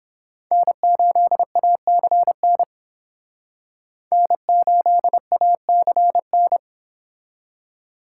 Morse code, heard as a keyed tone, about 30 wpm, 700 Hz. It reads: N8ACN N8ACN